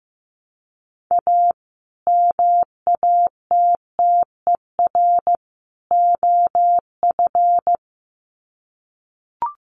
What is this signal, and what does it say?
Morse code, heard as a keyed tone, about 15 wpm, 700 Hz. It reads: A MATTER OF
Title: Sets of 3 Words 15wpm